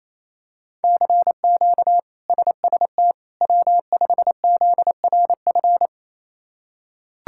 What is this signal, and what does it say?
Morse code, heard as a keyed tone, about 28 wpm, 700 Hz. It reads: CQ SST W5ZRF
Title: K1USN SST QSO Practice - 28wpm